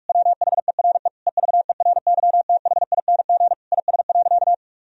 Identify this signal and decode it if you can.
Morse code, heard as a keyed tone, about 43 wpm, 700 Hz. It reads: WHERE EVERYTHING IS.
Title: Sentences from Top 1000 Words - Review Part 4 - 45wpm